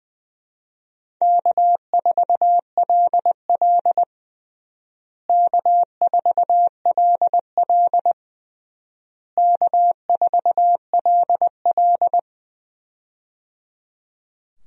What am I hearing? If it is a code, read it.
Morse code, heard as a keyed tone, about 20 wpm, 700 Hz. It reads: K4LL K4LL K4LL
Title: USA Call Signs - 4 Characters - Repeated 3x - 20wpm